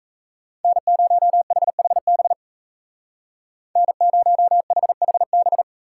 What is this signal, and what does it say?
Morse code, heard as a keyed tone, about 34 wpm, 700 Hz. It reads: N0HHB N0HHB
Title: Speed Racing - USA 1x3 Call Signs 28wpm